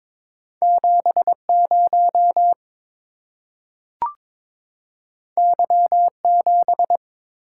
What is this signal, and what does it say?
Morse code, heard as a keyed tone, about 22 wpm, 700 Hz. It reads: 70 Y7
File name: practice / 22wpm / Two Letter-Number Combinations - 22wpm